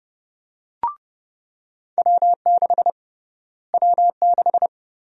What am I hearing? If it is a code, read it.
Morse code, heard as a keyed tone, about 30 wpm, 700 Hz. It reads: W6 W6
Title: USA Call Sign Prefixes - Repeated 2x - 30wpm